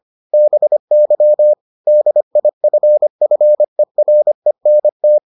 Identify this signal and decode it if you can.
Morse code, heard as a keyed tone, about 25 wpm, 600 Hz. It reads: BY DIFFERENT